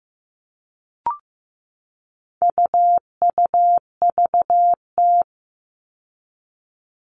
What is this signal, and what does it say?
Morse code, heard as a keyed tone, about 15 wpm, 700 Hz. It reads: UUVT
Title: All Cut Numbers - 4-Digit 15wpm